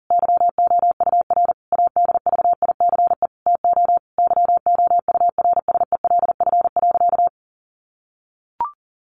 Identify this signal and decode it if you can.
Morse code, heard as a keyed tone, about 40 wpm, 700 Hz. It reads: YOUR ADVICE TO YOURSELF.
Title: Sentences from Top 1100 Words 40wpm